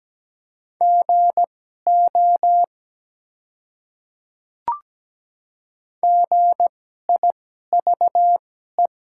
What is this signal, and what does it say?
Morse code, heard as a keyed tone, about 17 wpm, spaced as for 12 wpm, 700 Hz. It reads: GO GIVE
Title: 2x Character Spacing - Top 300 Words - 17wpm